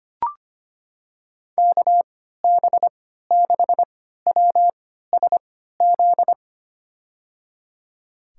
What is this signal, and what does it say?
Morse code, heard as a keyed tone, about 25 wpm, spaced as for 14 wpm, 700 Hz. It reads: KB6WSZ
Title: USA Call Signs - 6 Characters - 3x Character Spacing - 25wpm